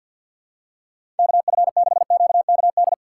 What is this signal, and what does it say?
Morse code, heard as a keyed tone, about 52 wpm, 700 Hz. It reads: KF6QKD
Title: USA Call Signs - 6 Characters - 6x Character Spacing - 50wpm